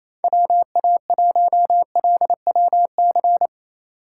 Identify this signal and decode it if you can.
Morse code, heard as a keyed tone, about 28 wpm, 700 Hz. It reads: WA1LWC
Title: K1USN SST QSO Practice - 28wpm